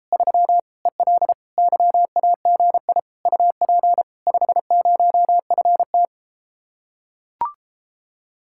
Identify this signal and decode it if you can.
Morse code, heard as a keyed tone, about 33 wpm, 700 Hz. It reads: 3 EL YAGI UP 50FT